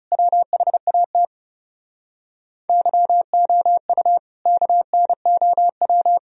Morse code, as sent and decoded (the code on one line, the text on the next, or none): .-- .... .- - / -.-- --- ..- / -.- -. --- .--
WHAT YOU KNOW